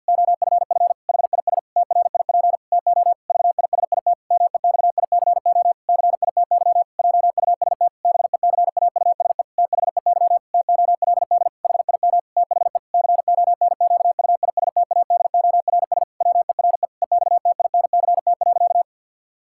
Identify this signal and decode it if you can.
Morse code, heard as a keyed tone, about 50 wpm, 700 Hz. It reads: OFF HIS TRIP TO VISIT MEXICO CITY JUST BECAUSE THEY TOLD HIM THE CONQUISTADORS WERE EXTINCT.
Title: Pangram Challenge v2 - 50wpm